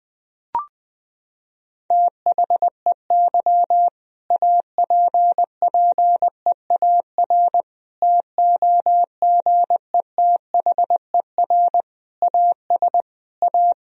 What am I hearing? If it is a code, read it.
Morse code, heard as a keyed tone, about 20 wpm, 700 Hz. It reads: THEY APPEAR TOGETHER AS A
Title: Sentences from Top 1000 Words - Review Part 2 - 20wpm